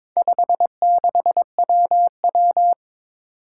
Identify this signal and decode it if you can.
Morse code, heard as a keyed tone, about 22 wpm, 700 Hz. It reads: 56WW
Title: Four Challenging Letter-Number Combinations 22wpm